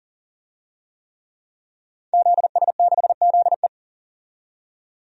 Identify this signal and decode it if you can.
Morse code, heard as a keyed tone, about 40 wpm, 700 Hz. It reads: ZS6ZE
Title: DXCC-100 Call Signs - 5 Characters - 40wpm